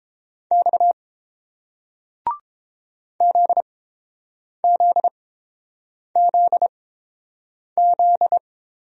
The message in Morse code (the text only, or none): X ZZZZ